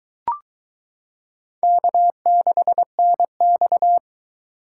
Morse code, heard as a keyed tone, about 22 wpm, 700 Hz. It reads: K6NX